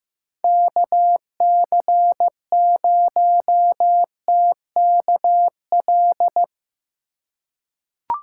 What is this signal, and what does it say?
Morse code, heard as a keyed tone, about 15 wpm, 700 Hz. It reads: KC0TKL
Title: USA Call Signs - 6 Characters - 8x Character Spacing - 15wpm